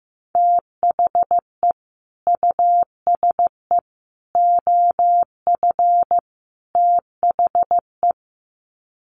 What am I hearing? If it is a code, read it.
Morse code, heard as a keyed tone, about 15 wpm, 700 Hz. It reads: THE USE OF THE
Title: Sets of 4 Words 15wpm